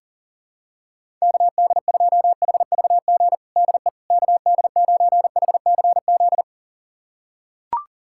Morse code, heard as a keyed tone, about 40 wpm, 700 Hz. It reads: KD2HVG DE KD9HCZ
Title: USA Callsign DE Callsign - 6 Characters - 40wpm